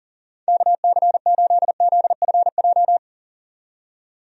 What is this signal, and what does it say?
Morse code, heard as a keyed tone, about 40 wpm, 700 Hz. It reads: KC8ZFJ
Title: USA Call Signs - 6 Characters - 40wpm